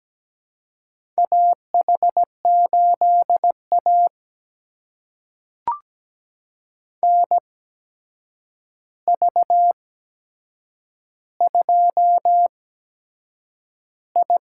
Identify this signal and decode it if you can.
Morse code, heard as a keyed tone, about 17 wpm, spaced as for 7 wpm, 700 Hz. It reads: AH8A NV2I
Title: USA Call Signs - 4 Characters - 8x Character Spacing - 17wpm